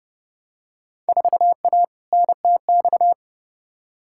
Morse code, heard as a keyed tone, about 32 wpm, 700 Hz. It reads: 4A NTX
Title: Field Day Exchanges - 30wpm